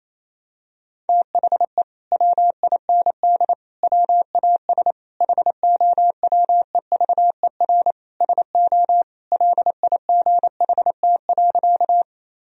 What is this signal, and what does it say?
Morse code, heard as a keyed tone, about 28 wpm, 700 Hz. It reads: THE WIND WAS HOWEVER SO LIGHT.
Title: Sentences from Top 1000 Words - Review Part 2 - 28wpm